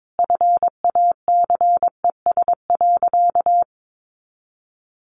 Morse code, heard as a keyed tone, about 22 wpm, 700 Hz. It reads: FACES.
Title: A Sound Of Thunder 22wpm